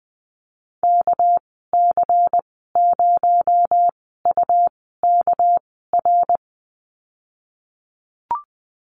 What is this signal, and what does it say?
Morse code, heard as a keyed tone, about 20 wpm, spaced as for 15 wpm, 700 Hz. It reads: KC0UKR
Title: 219 Lesson 35g - Number 0 - Call Signs 20wpm